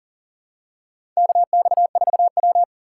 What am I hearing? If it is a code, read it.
Morse code, heard as a keyed tone, about 40 wpm, 690 Hz. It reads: KX4W
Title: USA Call Signs - 4 Characters - 2x Character Spacing - 40wpm